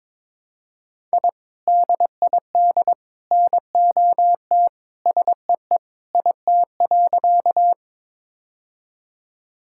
Morse code, heard as a keyed tone, about 22 wpm, 700 Hz. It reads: I DID NOT SEE IT.
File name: practice / 22wpm / Sentences with 3-Word Phrases - 22wpm